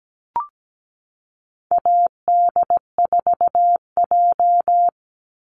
Morse code, heard as a keyed tone, about 17 wpm, 700 Hz. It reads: AD4J
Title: Call Signs Repeated 3x - 17wpm